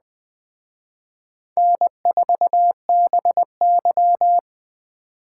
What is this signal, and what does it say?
Morse code, heard as a keyed tone, about 20 wpm, 700 Hz. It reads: N4BY
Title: Speed Racing - USA 1x2 Call Signs 17wpm